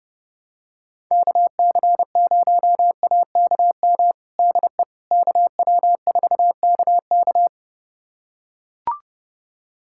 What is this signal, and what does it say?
Morse code, heard as a keyed tone, about 30 wpm, 700 Hz. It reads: KC0AKM DE KW4KK
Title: USA Callsign DE Callsign 30wpm